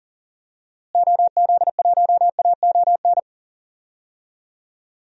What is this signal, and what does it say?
Morse code, heard as a keyed tone, about 37 wpm, 700 Hz. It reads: OZ1AON
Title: DXCC-100 Call Signs - 6 Characters - Repeated 3x - 40wpm